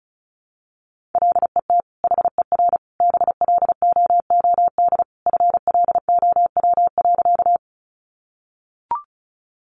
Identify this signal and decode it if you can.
Morse code, heard as a keyed tone, about 34 wpm, 700 Hz. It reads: LET HER BLOOD FLOW.
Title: Sentences from Top 1200 Words 35wpm